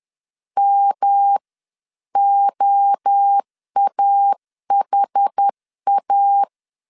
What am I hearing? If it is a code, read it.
Morse code, heard as a keyed tone, about 10 wpm, spaced as for 9 wpm, 800 Hz. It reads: MOAHA
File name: 10wpm